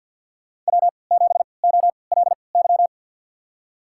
Morse code, heard as a keyed tone, about 53 wpm, spaced as for 29 wpm, 700 Hz. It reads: W7OLY
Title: USA Call Signs - 5 Characters - 3x Character Spacing - 50wpm